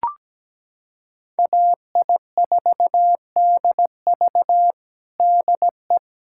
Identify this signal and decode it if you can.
Morse code, heard as a keyed tone, about 17 wpm, 700 Hz. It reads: AI4DV DE
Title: USA Callsign DE Callsign - 5 Characters - 17wpm